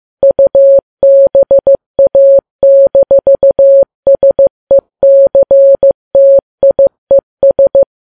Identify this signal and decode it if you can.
Morse code, heard as a keyed tone, about 15 wpm, 550 Hz. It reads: UBA-SECTIES
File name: Files Web tls / Morse on9fv / morse15kps